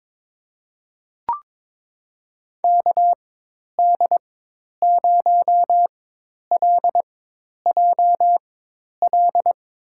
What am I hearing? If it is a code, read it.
Morse code, heard as a keyed tone, about 22 wpm, spaced as for 10 wpm, 700 Hz. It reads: KD0LJL
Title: USA Call Signs - 4x Character Spacing - 22wpm